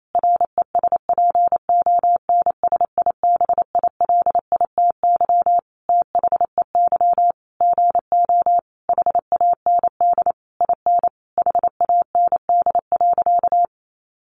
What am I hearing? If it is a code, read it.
Morse code, heard as a keyed tone, about 28 wpm, 700 Hz. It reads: RESPONSIBILITY THEY GO HAND IN HAND.
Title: Sentences from Top 1200 Words 28wpm